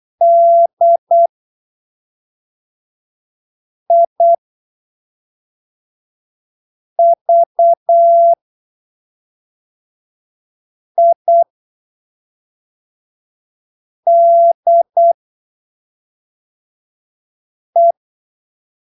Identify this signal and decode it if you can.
Morse code, heard as a keyed tone, about 8 wpm, spaced as for 3 wpm, 680 Hz. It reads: DIVIDE
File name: arquivos / texto / texto 29 08ppm